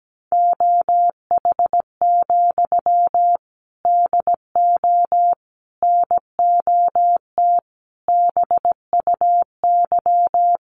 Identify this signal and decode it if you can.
Morse code, heard as a keyed tone, about 17 wpm, 700 Hz. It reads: OH, DO NOT BUY
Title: Sentences from Top 1300 Words 17wpm